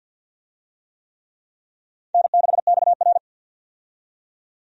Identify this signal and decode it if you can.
Morse code, heard as a keyed tone, about 51 wpm, 700 Hz. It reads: N6XR
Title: USA Call Signs - 4 Characters - 50wpm